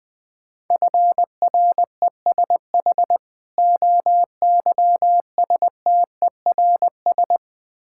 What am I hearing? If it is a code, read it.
Morse code, heard as a keyed tone, about 20 wpm, 700 Hz. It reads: FRESH OYSTERS
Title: Difficult Words with Hints - 20wpm